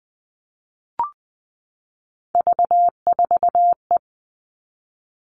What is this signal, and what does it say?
Morse code, heard as a keyed tone, about 20 wpm, 700 Hz. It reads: V4E
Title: All Cut Numbers - 3-Digit 20wpm